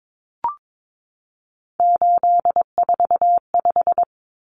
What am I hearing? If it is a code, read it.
Morse code, heard as a keyed tone, about 22 wpm, 700 Hz. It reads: 845
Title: Common Cut Numbers - 3-Digit 22wpm